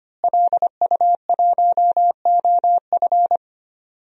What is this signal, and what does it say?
Morse code, heard as a keyed tone, about 25 wpm, 700 Hz. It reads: LU1OF
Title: DXCC-100 Call Signs - 5 Characters - Repeated 3x - 25wpm